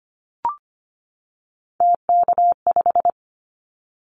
Morse code, heard as a keyed tone, about 25 wpm, 700 Hz. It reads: TK5
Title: Callsign Prefixes for Top 100 DXCC - 25wpm